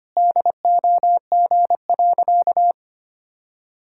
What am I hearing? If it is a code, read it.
Morse code, heard as a keyed tone, about 25 wpm, 700 Hz. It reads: DOG.
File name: practice / 25wpm / Sentences with 2-Word Phrases - 25wpm